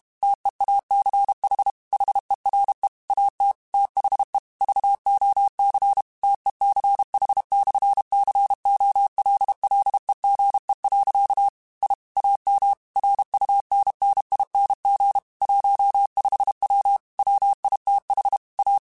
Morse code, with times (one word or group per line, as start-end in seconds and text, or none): 0.2-1.7 TEACH
1.9-2.9 HERE
3.1-3.5 AT
3.7-4.4 THE
4.6-6.0 VOC
6.2-11.5 TECH/COLLEGE.
11.8-11.9 I
12.2-12.7 AM
13.0-15.2 RUNNING
15.4-17.0 15W
17.2-18.4 WITH
18.6-18.8 A